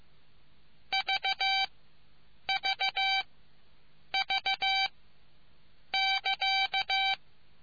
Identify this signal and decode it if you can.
Morse code, heard as a keyed tone, about 15 wpm, spaced as for 7 wpm, 770 Hz. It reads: VVV <KA>